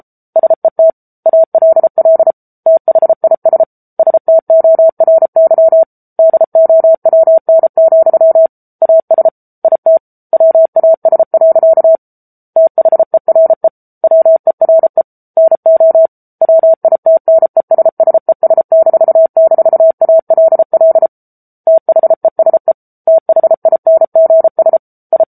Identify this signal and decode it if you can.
Morse code, heard as a keyed tone, about 34 wpm, 650 Hz. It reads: SET ALL THIS STORY DOWN, AS IT WAS. THERE WERE NO WITNESSES--ALL THESE THINGS I
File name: Book Two / 07-Chapter 7 - 33 WPM